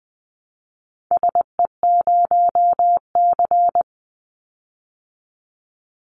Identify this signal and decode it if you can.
Morse code, heard as a keyed tone, about 20 wpm, 700 Hz. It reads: SE0C
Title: Call Signs Repeated 3x - 20wpm